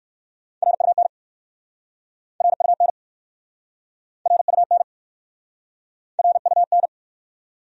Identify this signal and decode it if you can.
Morse code, heard as a keyed tone, about 62 wpm, 700 Hz. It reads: RUN RUN RUN RUN